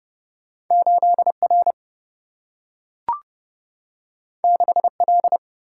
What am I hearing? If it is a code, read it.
Morse code, heard as a keyed tone, about 30 wpm, 700 Hz. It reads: 8 R 6 L
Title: Single-Letter Single-Number 30wpm